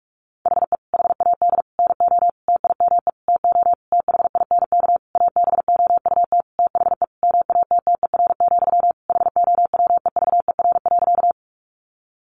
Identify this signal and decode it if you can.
Morse code, heard as a keyed tone, about 44 wpm, 700 Hz. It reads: HE HAD NO TIME TO THINK ABOUT THE MATTER, HOWEVER.
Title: Sentences from Top 700 Words 45wpm